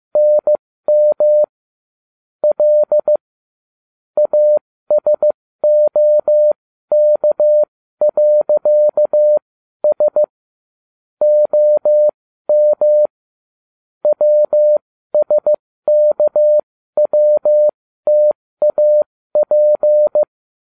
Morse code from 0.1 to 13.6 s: NM L ASOK.S OM

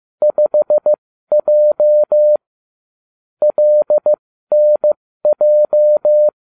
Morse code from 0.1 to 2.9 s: 5J

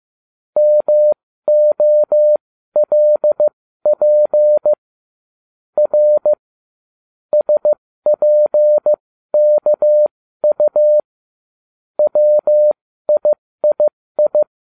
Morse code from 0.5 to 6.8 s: MOLP R

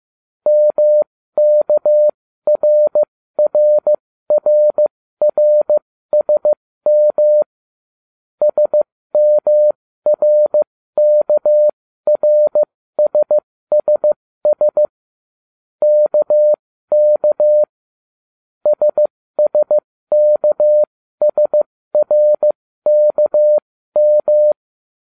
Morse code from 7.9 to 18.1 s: SMRKRSSS KK

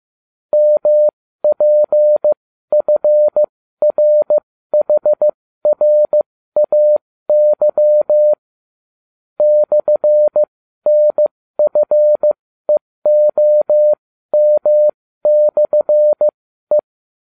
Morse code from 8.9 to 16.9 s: /NFEOM/E

